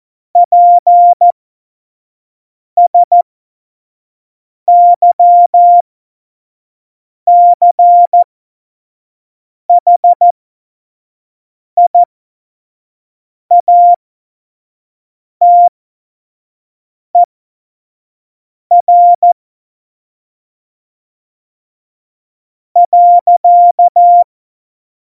Morse code, heard as a keyed tone, about 14 wpm, spaced as for 5 wpm, 700 Hz. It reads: PSYCHIATER .
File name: Morsekurs / Witze / Witz 09. Januar 2018, 25 F70